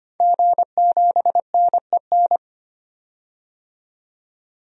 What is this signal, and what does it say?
Morse code, heard as a keyed tone, about 25 wpm, 700 Hz. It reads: G7NEN
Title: DXCC-100 Call Signs - 5 Characters - 25wpm